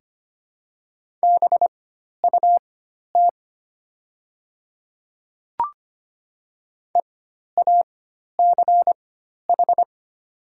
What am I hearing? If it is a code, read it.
Morse code, heard as a keyed tone, about 25 wpm, spaced as for 12 wpm, 700 Hz. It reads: BUT EACH
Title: 4x Character Spacing - Top 100 Words - 25wpm